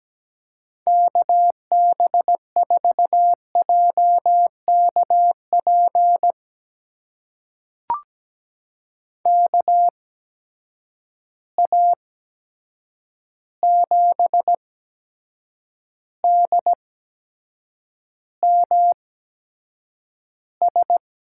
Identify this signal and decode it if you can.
Morse code, heard as a keyed tone, about 17 wpm, spaced as for 7 wpm, 700 Hz. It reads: KB4JKP KA7DMS
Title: USA Call Signs - 6 Characters - 8x Character Spacing - 17wpm